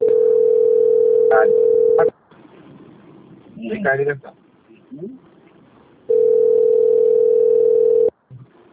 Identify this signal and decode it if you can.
Ringback tone (North America), 440 + 480 Hz, with speech.
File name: out- - - - - .30956